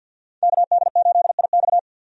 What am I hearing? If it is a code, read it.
Morse code, heard as a keyed tone, about 48 wpm, 700 Hz. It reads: KD8IX
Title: USA Call Signs - 5 Characters - 50wpm